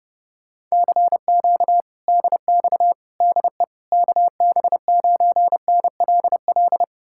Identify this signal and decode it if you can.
Morse code, heard as a keyed tone, about 34 wpm, 700 Hz. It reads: CQ DX DE KB9NLL
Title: Speed-Racing Common QSO Phrases - 30wpm